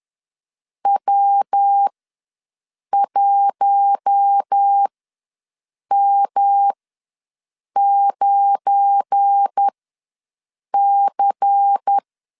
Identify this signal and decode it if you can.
Morse code, heard as a keyed tone, about 10 wpm, spaced as for 6 wpm, 800 Hz. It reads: W1M9C
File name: LEZIONI / 7 GF9Z / 7 GF9 ws 10wpm 4034